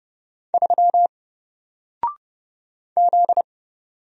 Morse code, heard as a keyed tone, about 29 wpm, 700 Hz. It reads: 3 Z
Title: Single Letter-Number 30wpm